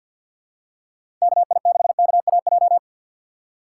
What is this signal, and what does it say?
Morse code, heard as a keyed tone, about 52 wpm, 700 Hz. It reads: KI6KAJ
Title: USA Call Signs - 6 Characters - 3x Character Spacing - 50wpm